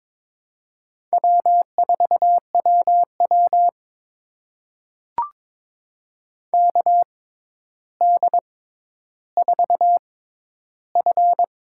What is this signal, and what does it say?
Morse code, heard as a keyed tone, about 22 wpm, spaced as for 11 wpm, 700 Hz. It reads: W4WW KD4F
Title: USA Call Signs - 6x Character Spacing - 22wpm